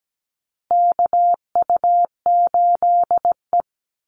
Morse code, heard as a keyed tone, about 17 wpm, 700 Hz. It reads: KU8E
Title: Call Signs Repeated 3x - 17wpm